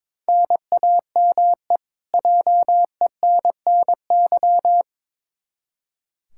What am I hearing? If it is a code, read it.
Morse code, heard as a keyed tone, about 22 wpm, 700 Hz. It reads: NAME JENNY
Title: K1USN SST QSO Practice 22wpm